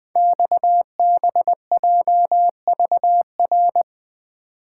Morse code, heard as a keyed tone, about 20 wpm, 700 Hz. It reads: XBJVR
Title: Code Groups - 5 Letters - 20wpm